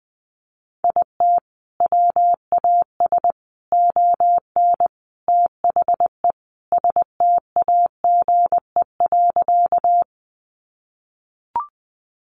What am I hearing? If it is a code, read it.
Morse code, heard as a keyed tone, about 20 wpm, 700 Hz. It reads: IT WAS ON THE STAGE.
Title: Sentences from Top 1200 Words 20wpm